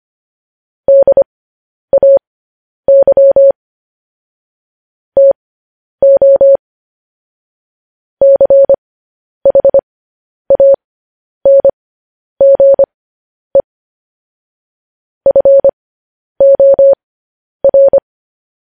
Morse code, heard as a keyed tone, about 25 wpm, spaced as for 10 wpm, 550 Hz. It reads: DAY TO CHANGE FOR
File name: podcast / News headlines-1x-25x10